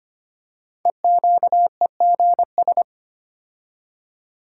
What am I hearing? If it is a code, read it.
Morse code, heard as a keyed tone, about 25 wpm, 700 Hz. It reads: EQEGS
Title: Code Groups - 5 Letters - 25wpm